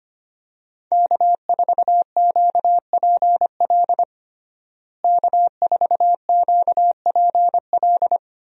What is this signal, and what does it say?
Morse code, heard as a keyed tone, about 25 wpm, 700 Hz. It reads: K4QPL K4QPL
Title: USA Call Signs - 5 Characters - Repeated 2x - 25wpm